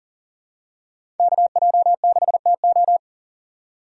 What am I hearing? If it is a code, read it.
Morse code, heard as a keyed tone, about 38 wpm, 690 Hz. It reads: KJ6TO
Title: USA Call Signs - 5 Characters - 3x Character Spacing - 40wpm